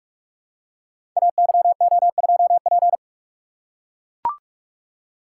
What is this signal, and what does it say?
Morse code, heard as a keyed tone, about 44 wpm, 700 Hz. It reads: AYO2P
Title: Code Groups - 5 Letter:Numbers - 45wpm